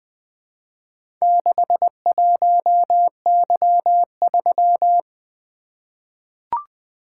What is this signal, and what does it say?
Morse code, heard as a keyed tone, about 20 wpm, 700 Hz. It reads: 61Y3
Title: Four Letter-Number Combinations - 20wpm